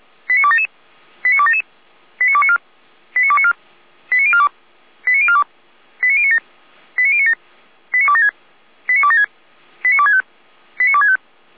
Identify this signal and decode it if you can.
Five-tone selective call (ZVEI), heard as quick runs of five tones, tones of about 70 ms. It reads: 89277 89277 89294 89294 89042 89042 89097 89097 89276 89276 89265 89265
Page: /455/SDR/ZVEI/